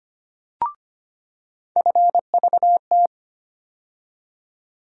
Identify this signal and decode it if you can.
Morse code, heard as a keyed tone, about 25 wpm, 700 Hz. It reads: FVT
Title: USA Call Sign Postfixes - 3 Letters 25wpm